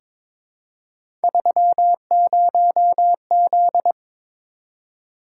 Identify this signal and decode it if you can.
Morse code, heard as a keyed tone, about 22 wpm, 700 Hz. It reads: 30Z
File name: practice / 22wpm / Three Letter-Number Combinations - 22wpm